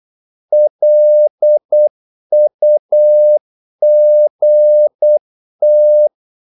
Morse code, heard as a keyed tone, about 8 wpm, 600 Hz. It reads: LUGT
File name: Korte ord DK 8wpm